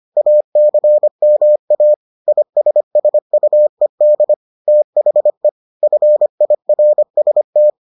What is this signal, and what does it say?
Morse code, heard as a keyed tone, about 25 wpm, 600 Hz. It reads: ACMA ISSUED THE FIRST